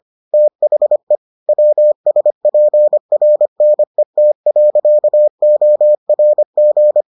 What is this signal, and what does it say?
Morse code, heard as a keyed tone, about 25 wpm, 600 Hz. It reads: THE WSPRNET.ORG